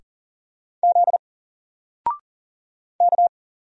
Morse code, heard as a keyed tone, about 40 wpm, 700 Hz. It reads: Z K
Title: Single Letter-Number Alt 40wpm